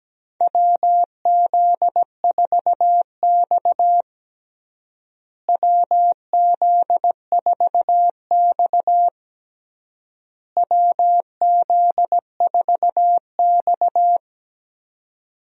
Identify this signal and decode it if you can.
Morse code, heard as a keyed tone, about 17 wpm, 700 Hz. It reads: WZ4X WZ4X WZ4X
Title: USA Call Signs - 4 Characters - Repeated 3x - 17wpm